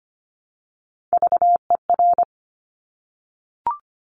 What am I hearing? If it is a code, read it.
Morse code, heard as a keyed tone, about 25 wpm, 700 Hz. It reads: VER
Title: Top Three Letter Combinations in Words 25wpm